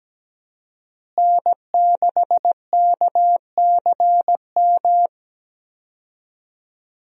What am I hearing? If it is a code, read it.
Morse code, heard as a keyed tone, about 17 wpm, 700 Hz. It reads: N6KCM
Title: USA Call Signs - 5 Characters - Repeated 3x - 17wpm